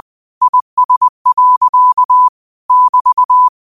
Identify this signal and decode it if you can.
Morse code, heard as a keyed tone, about 20 wpm, 1000 Hz. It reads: IS. <BT>
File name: qotdfiles / 20wpm